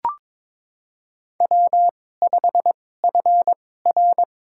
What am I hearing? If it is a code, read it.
Morse code, heard as a keyed tone, about 22 wpm, spaced as for 16 wpm, 700 Hz. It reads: W5FR
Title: USA Call Signs - 4 Characters - 2x Character Spacing - 22wpm